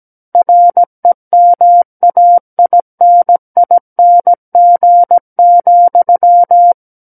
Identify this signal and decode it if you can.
Morse code, heard as a keyed tone, about 17 wpm, 700 Hz. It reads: REMAINING,
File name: Book One / 08-Chapter 8 - 17 WPM